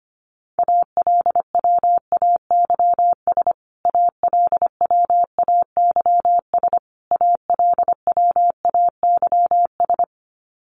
Morse code, heard as a keyed tone, about 25 wpm, 700 Hz. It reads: ALWAYS ALWAYS ALWAYS
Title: Top 500 Words Repeated 25wpm